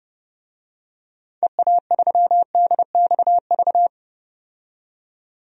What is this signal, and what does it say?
Morse code, heard as a keyed tone, about 29 wpm, 700 Hz. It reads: EA3DXV
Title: DXCC-100 Call Signs - 6 Characters - 30wpm